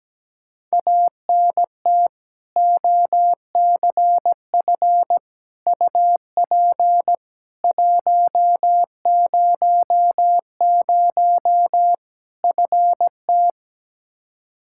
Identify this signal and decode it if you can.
Morse code, heard as a keyed tone, about 17 wpm, 700 Hz. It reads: ANT OCF UP 100 FT
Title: K1USN SST QSO Practice 17wpm